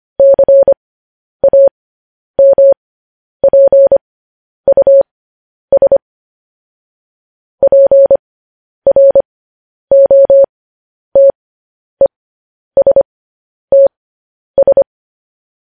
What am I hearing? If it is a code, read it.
Morse code, heard as a keyed tone, about 25 wpm, spaced as for 10 wpm, 550 Hz. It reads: CAMPUS PROTESTS